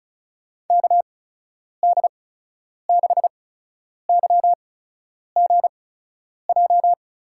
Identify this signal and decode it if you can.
Morse code, heard as a keyed tone, about 36 wpm, spaced as for 10 wpm, 700 Hz. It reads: KD6YGJ
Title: USA Call Signs - 6 Characters - 8x Character Spacing - 35wpm